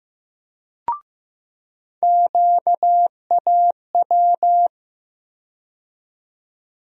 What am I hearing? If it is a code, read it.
Morse code, heard as a keyed tone, about 16 wpm, 700 Hz. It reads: QAW
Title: USA Call Sign Postfixes - 3 Letters 15wpm